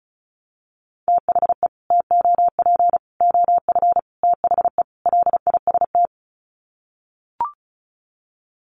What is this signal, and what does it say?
Morse code, heard as a keyed tone, about 34 wpm, 700 Hz. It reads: THE TOP OF THE LIST
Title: Sets of 5 Words Encore2 35wpm